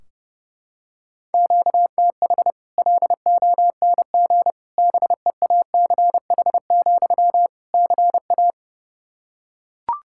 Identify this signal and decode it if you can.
Morse code, heard as a keyed tone, about 30 wpm, 700 Hz. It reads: QTH LONG BEACH, CA
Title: Practice QSO - Exchange 1 - Beginning - 30wpm